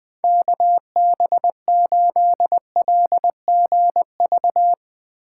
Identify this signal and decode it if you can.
Morse code, heard as a keyed tone, about 21 wpm, 700 Hz. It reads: KB8LGV
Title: USA Call Signs - 3x Character Spacing - 20wpm